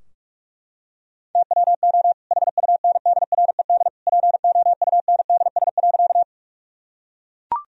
Morse code, heard as a keyed tone, about 42 wpm, 700 Hz. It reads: TWO HUNDRED POUNDS.
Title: Sentences from Top 1000 Words - Review Part 2 - 45wpm